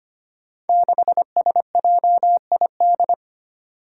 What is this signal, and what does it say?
Morse code, heard as a keyed tone, about 25 wpm, 700 Hz. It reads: 6SJID
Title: Five Challenging Letter-Number Combinations 25wpm